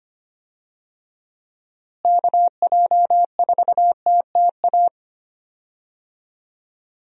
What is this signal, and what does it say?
Morse code, heard as a keyed tone, about 25 wpm, 700 Hz. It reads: KJ4TTA
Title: USA Call Signs - 6 Characters - Repeated 3x - 25wpm